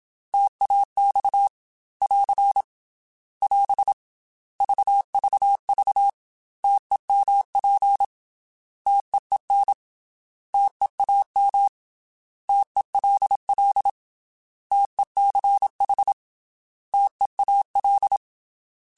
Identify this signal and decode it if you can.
Morse code, heard as a keyed tone, about 27 wpm, 800 Hz. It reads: TAX <AR> <AS> VVV TEMP TEEN TEAM TELL TECH TEAL
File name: T Prefixes / T mp3s